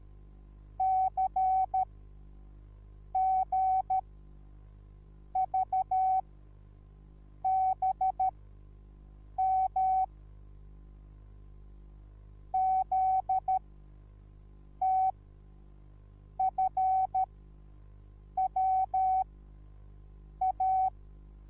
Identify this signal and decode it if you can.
Morse code, heard as a keyed tone, about 13 wpm, spaced as for 6 wpm, 760 Hz. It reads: CGVBM ZTFWA